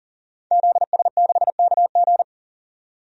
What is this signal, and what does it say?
Morse code, heard as a keyed tone, about 40 wpm, 700 Hz. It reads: ZS6KG
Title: DXCC-100 Call Signs - 5 Characters - 40wpm